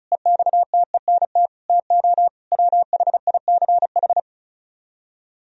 Morse code, heard as a keyed tone, about 44 wpm, 700 Hz. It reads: EXTENT TO WHICH